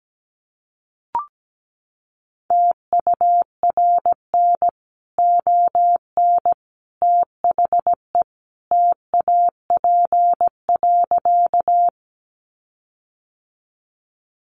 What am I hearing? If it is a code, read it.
Morse code, heard as a keyed tone, about 17 wpm, 700 Hz. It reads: TURN ON THE TAP.
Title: Sentences from Top 1400 Words 17wpm